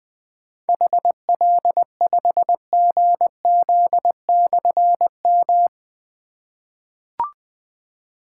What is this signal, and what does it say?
Morse code, heard as a keyed tone, about 20 wpm, 700 Hz. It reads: HL5GZ/M
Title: Difficult Call Signs 20wpm